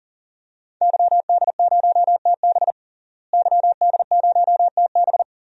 Morse code, heard as a keyed tone, about 40 wpm, 700 Hz. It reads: YD0TB YD0TB
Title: DXCC-100 Call Signs - 5 Characters - Repeated 2x - 40wpm